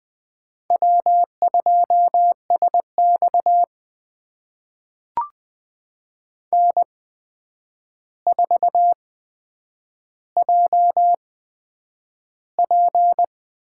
Morse code, heard as a keyed tone, about 20 wpm, spaced as for 9 wpm, 700 Hz. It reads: W2SX N4JP
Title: USA Call Signs - 4 Characters - 8x Character Spacing - 20wpm